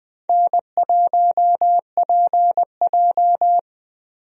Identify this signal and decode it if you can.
Morse code, heard as a keyed tone, about 20 wpm, 700 Hz. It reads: N1PJ
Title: USA Call Signs - 4 Characters - Repeated 2x - 20wpm